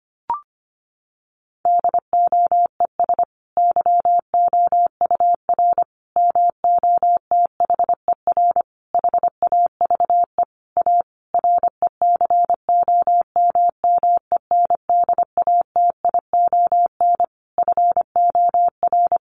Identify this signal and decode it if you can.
Morse code, heard as a keyed tone, about 25 wpm, 700 Hz. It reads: DOES YOUR MOTHER HAVE A RECOMMENDATION FOR
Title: Sentences from Top 1400 Words 25wpm